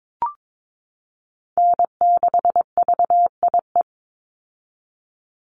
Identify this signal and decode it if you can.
Morse code, heard as a keyed tone, about 22 wpm, 700 Hz. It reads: N6VIE
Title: USA Call Signs 22wpm